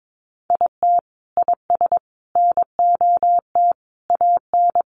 Morse code, heard as a keyed tone, about 22 wpm, 700 Hz. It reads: IT IS NOT AN
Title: Sentences from Top 1500 Words 22wpm